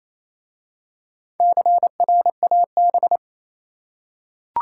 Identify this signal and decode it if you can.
Morse code, heard as a keyed tone, about 27 wpm, 700 Hz. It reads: CRAB
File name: Easy Words - Speed-Racing with Hints - 22wpm